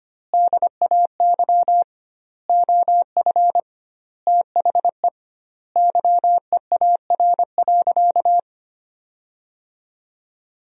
Morse code, heard as a keyed tone, about 25 wpm, 700 Hz. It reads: DAY OF THE YEAR.
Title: 2x Word-Spacing - Sentences from Top 300 Words - 25wpm